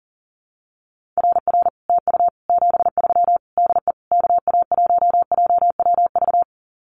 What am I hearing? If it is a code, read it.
Morse code, heard as a keyed tone, about 43 wpm, 700 Hz. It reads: RR TU 73 DE KA1JWV
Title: Common QSO Phrases Encore 40wpm